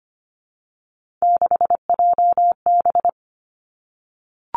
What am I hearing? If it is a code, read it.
Morse code, heard as a keyed tone, about 25 wpm, 700 Hz. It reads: 6JB
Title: Three Challenging Letter-Number Combinations 25wpm